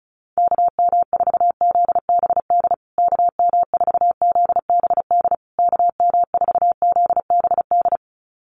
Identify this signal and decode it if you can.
Morse code, heard as a keyed tone, about 34 wpm, 700 Hz. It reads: KM4ZBD KM4ZBD KM4ZBD
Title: USA Call Signs Repeated 3x 35wpm